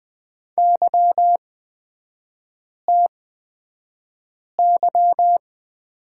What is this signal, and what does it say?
Morse code, heard as a keyed tone, about 20 wpm, spaced as for 5 wpm, 700 Hz. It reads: YTY
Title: 224 Lesson 36e - BT - Three Characters 20wpm